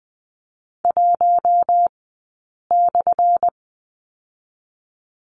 Morse code, heard as a keyed tone, about 20 wpm, 700 Hz. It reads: 1 /
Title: 177 Lesson 29d - Slash - Two Characters 20wpm